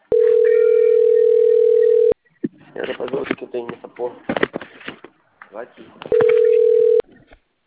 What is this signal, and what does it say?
Ringback tone (North America), 440 + 480 Hz, with speech.
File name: out- - - - - .51983